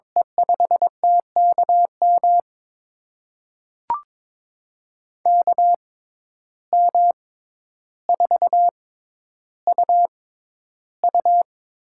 Morse code, heard as a keyed tone, about 22 wpm, spaced as for 11 wpm, 700 Hz. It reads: E5TKM KM4UU
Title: USA Call Signs - 6 Characters - 6x Character Spacing - 22wpm